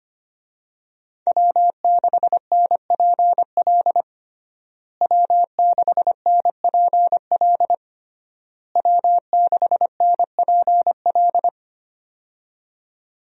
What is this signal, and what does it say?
Morse code, heard as a keyed tone, about 25 wpm, 700 Hz. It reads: W6NPL W6NPL W6NPL
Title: USA Call Signs - 5 Characters - Repeated 3x - 25wpm